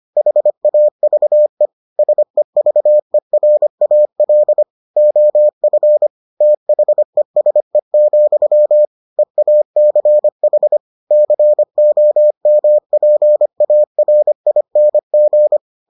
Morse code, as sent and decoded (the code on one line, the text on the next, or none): .... .- ...- . / ... ...- . .-. .- .-.. / --- ..-. / - .... ... --..-- / . .- -.-. .... / -.-. --- -- .--. .- .-. .. -. --.
HAVE SEVERAL OF THESE, EACH COMPARING